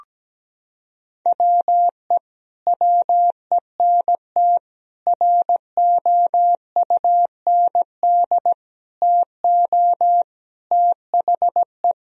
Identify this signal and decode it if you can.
Morse code, heard as a keyed tone, about 17 wpm, 700 Hz. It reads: WE WENT ROUND TO THE
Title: Sentences from Top 1000 Words - Review Part 1 - 17wpm